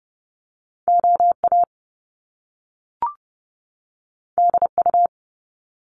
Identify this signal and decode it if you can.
Morse code, heard as a keyed tone, about 30 wpm, 700 Hz. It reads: OA DU
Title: Top Two Letter Combinations in Words 30wpm